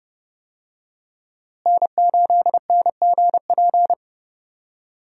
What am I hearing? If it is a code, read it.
Morse code, heard as a keyed tone, about 30 wpm, 700 Hz. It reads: N8NGP
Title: USA Call Signs - 5 Characters - Repeated 2x - 30wpm